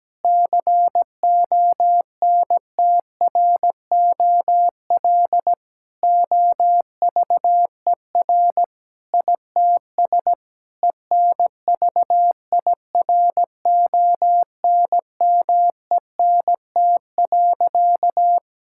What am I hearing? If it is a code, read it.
Morse code, heard as a keyed tone, about 17 wpm, 700 Hz. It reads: CONTROL OVER ITS ENVIRONMENT.
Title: Sentences from Top 1000 Words - Review Part 5 - 17wpm